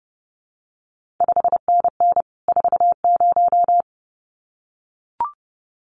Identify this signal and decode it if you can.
Morse code, heard as a keyed tone, about 30 wpm, 700 Hz. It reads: 5NN 40
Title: CQ WW DX Contest Exchange Encore - 30wpm